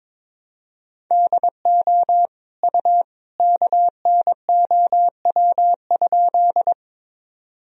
Morse code, heard as a keyed tone, about 22 wpm, 700 Hz. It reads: DO U KNOW?
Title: Sentences from Top 100 Words and Abbreviations 22wpm